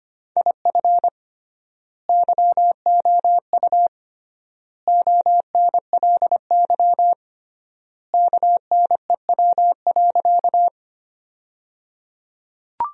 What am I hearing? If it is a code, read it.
Morse code, heard as a keyed tone, about 25 wpm, 700 Hz. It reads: IF YOU ONLY KNEW.
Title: 3x Word-Spacing - Sentences from Top 400 Words - 25wpm